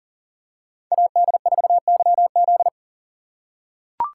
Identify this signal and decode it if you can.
Morse code, heard as a keyed tone, about 40 wpm, 700 Hz. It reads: AD4YZ
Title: USA Call Signs - 5 Characters - 3x Character Spacing - 40wpm